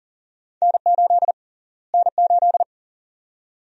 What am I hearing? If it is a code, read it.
Morse code, heard as a keyed tone, about 40 wpm, 700 Hz. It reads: N8 N8
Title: USA Call Sign Prefixes - Repeated 2x - 40wpm